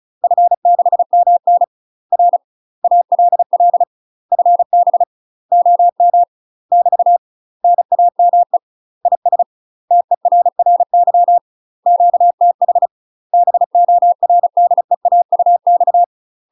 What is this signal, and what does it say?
Morse code, heard as a keyed tone, about 34 wpm, 700 Hz. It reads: F6MN R ALL FB OM <BT> NAME IS TERRY QTH BORDEAUX
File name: UK MASTER QSOs TRAINING / 35WPM / UK MASTER QSOs 0008